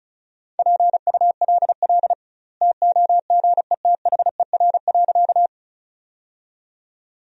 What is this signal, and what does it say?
Morse code, heard as a keyed tone, about 36 wpm, 700 Hz. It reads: PULL TOGETHER.
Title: 2x Word-Spacing - Sentences from Top 400 Words - 35wpm